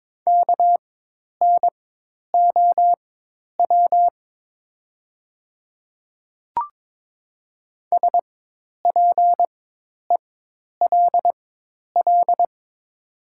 Morse code, heard as a keyed tone, about 22 wpm, spaced as for 10 wpm, 700 Hz. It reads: KNOW SPELL
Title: 4x Character Spacing - Top 300 Words - 22wpm